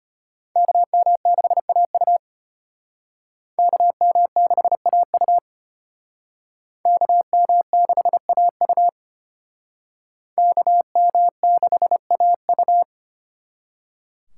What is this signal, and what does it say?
Morse code, heard as a keyed tone, about 32 wpm, 700 Hz. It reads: KM6AU KM6AU KM6AU KM6AU